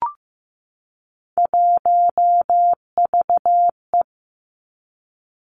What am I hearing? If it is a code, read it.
Morse code, heard as a keyed tone, about 15 wpm, 700 Hz. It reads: 1VE
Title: Three Challenging Letter-Number Combinations 15wpm